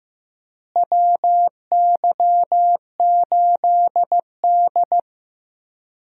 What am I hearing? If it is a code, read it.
Morse code, heard as a keyed tone, about 15 wpm, 700 Hz. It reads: WY8D
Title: USA Call Signs - 4 Characters - Repeated 3x - 15wpm